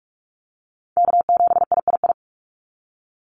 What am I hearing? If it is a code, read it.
Morse code, heard as a keyed tone, about 46 wpm, 690 Hz. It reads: K7III
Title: NA SOTA Chaser Callsigns 45wpm